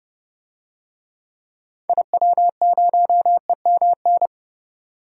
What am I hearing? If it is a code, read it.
Morse code, heard as a keyed tone, about 32 wpm, 700 Hz. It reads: IW0EMN
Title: DXCC-100 Call Signs - 6 Characters - 30wpm